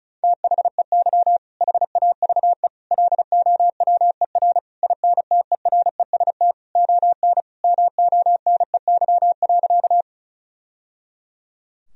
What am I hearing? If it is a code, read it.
Morse code, heard as a keyed tone, about 36 wpm, 700 Hz. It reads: THEY HAVE LOWER INTEREST ON MONEY.
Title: Sentences from Top 1000 Words - Review Part 4 - 35wpm